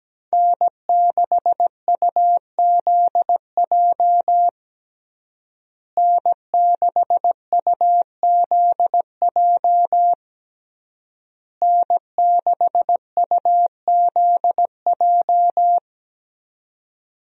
Morse code, heard as a keyed tone, about 17 wpm, 700 Hz. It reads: N6UZJ N6UZJ N6UZJ
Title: USA Call Signs - 5 Characters - Repeated 3x - 17wpm